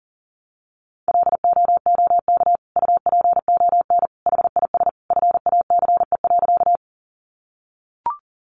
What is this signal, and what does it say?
Morse code, heard as a keyed tone, about 37 wpm, 700 Hz. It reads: LOOK UPON HIS FACE.
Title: Sentences from Top 1100 Words 40wpm